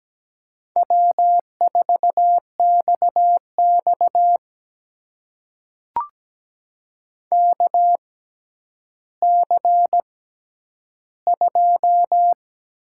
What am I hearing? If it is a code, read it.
Morse code, heard as a keyed tone, about 17 wpm, spaced as for 10 wpm, 700 Hz. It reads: W4XX KC2
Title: USA Call Signs - 4 Characters - 6x Character Spacing - 17wpm